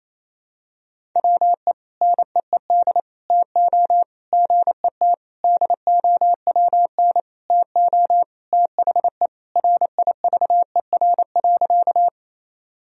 Morse code, heard as a keyed tone, about 28 wpm, 700 Hz. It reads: WE NEED TO GET DOWN TO THE RIVER.
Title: Sentences from Top 1000 Words - Review Part 1 - 28wpm